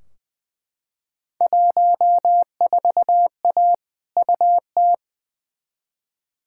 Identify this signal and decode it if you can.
Morse code, heard as a keyed tone, about 20 wpm, 700 Hz. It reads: 14A UT
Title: Field Day Exchanges Encore - 20wpm